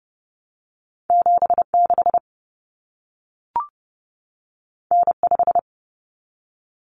Morse code, heard as a keyed tone, about 30 wpm, 700 Hz. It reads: 76 N5
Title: Common Cut Numbers - 2-Digit 30wpm